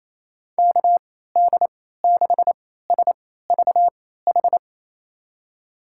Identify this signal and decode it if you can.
Morse code, heard as a keyed tone, about 28 wpm, spaced as for 16 wpm, 700 Hz. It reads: KD6SVH
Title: USA Call Signs - 6 Characters - 3x Character Spacing - 28wpm